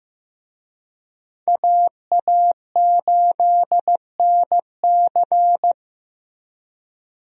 Morse code, heard as a keyed tone, about 16 wpm, 700 Hz. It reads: AA8NC
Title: USA Call Signs - 5 Characters - Repeated 3x - 15wpm